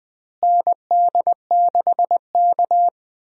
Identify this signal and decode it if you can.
Morse code, heard as a keyed tone, about 21 wpm, 700 Hz. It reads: ND6K
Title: USA Call Signs - 4 Characters - Repeated 3x - 20wpm